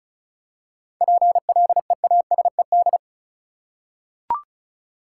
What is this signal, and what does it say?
Morse code, heard as a keyed tone, about 34 wpm, 700 Hz. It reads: PLEASED
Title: Seven Letter Words - 35wpm